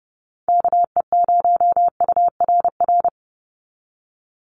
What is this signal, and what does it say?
Morse code, heard as a keyed tone, about 28 wpm, 700 Hz. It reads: KE0URR
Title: USA Call Signs 30wpm